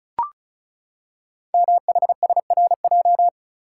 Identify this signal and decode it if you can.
Morse code, heard as a keyed tone, about 36 wpm, 690 Hz. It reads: MHSRJ
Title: Code Groups - 5 Letters - 35wpm